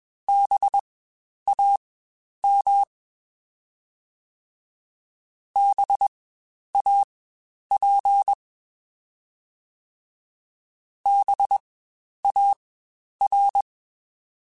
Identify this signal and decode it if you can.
Morse code, heard as a keyed tone, about 21 wpm, spaced as for 10 wpm, 800 Hz. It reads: BAM BAP BAR
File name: Spaced Alphas / Spaced Alphas mp3s / Bs sp @20